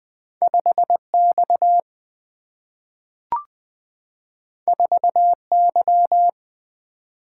Morse code, heard as a keyed tone, about 20 wpm, 700 Hz. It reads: 5 X 4 Y
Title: Single-Letter Single-Number 20wpm